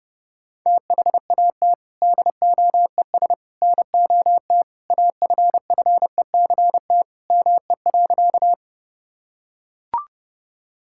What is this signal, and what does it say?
Morse code, heard as a keyed tone, about 30 wpm, 700 Hz. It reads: THAT DOES NOT AFFECT ME.
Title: Sentences from Top 1000 Words - Review Part 6 - 30wpm